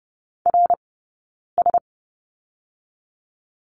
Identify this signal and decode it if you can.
Morse code, heard as a keyed tone, about 30 wpm, 700 Hz. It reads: R S
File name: 092 Lesson 16d - Letter c - Two Characters 30wpm